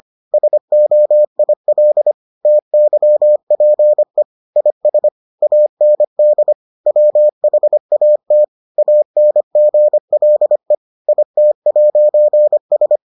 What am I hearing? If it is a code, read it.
Morse code, heard as a keyed tone, about 25 wpm, 600 Hz. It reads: SOIL TYPE IS AND WHAT ANGLE IT'S